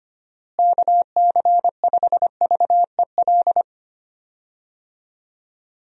Morse code, heard as a keyed tone, about 25 wpm, 700 Hz. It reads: KC5VEL
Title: USA Call Signs - 6 Characters - Repeated 3x - 25wpm